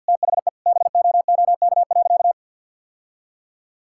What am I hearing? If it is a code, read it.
Morse code, heard as a keyed tone, about 52 wpm, 700 Hz. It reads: THE BOOK.
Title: Sentences from Top 1000 Words - Review Part 5 - 50wpm